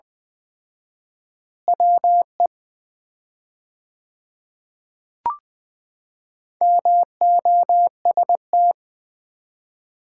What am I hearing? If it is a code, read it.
Morse code, heard as a keyed tone, about 20 wpm, 700 Hz. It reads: WE MOST
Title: 4x Word-Spacing - Sets of 2 Words - 20wpm